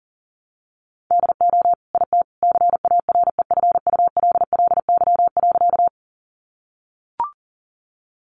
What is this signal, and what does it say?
Morse code, heard as a keyed tone, about 40 wpm, 700 Hz. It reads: DO IT CAREFULLY.
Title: Sentences from Top 1300 Words 40wpm